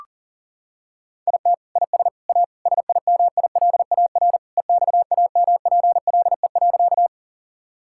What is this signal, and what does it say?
Morse code, heard as a keyed tone, about 37 wpm, 700 Hz. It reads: IT IS A SIMILAR EXAMPLE.
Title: Sentences from Top 1000 Words - Review Part 5 - 40wpm